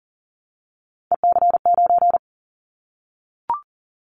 Morse code, heard as a keyed tone, about 42 wpm, 690 Hz. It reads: EC9
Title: Callsign Prefixes for Top 100 DXCC - 40wpm